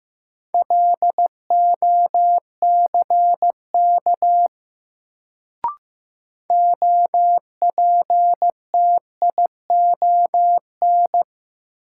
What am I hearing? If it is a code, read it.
Morse code, heard as a keyed tone, about 15 wpm, 700 Hz. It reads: LOCK OPTION
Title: Sentences from Top 1300 Words - New Word Drill Alt - 15wpm